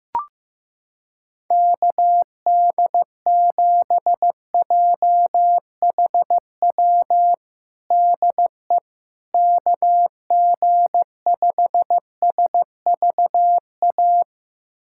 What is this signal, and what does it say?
Morse code, heard as a keyed tone, about 15 wpm, 700 Hz. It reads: KD7JHW DE KG5SVA
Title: USA Callsign DE Callsign - 6 Characters - 15wpm